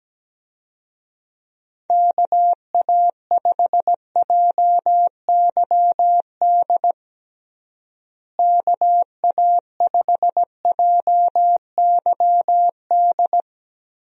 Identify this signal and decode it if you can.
Morse code, heard as a keyed tone, about 17 wpm, 700 Hz. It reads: KA5JYD KA5JYD
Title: USA Call Signs - 6 Characters - Repeated 2x - 17wpm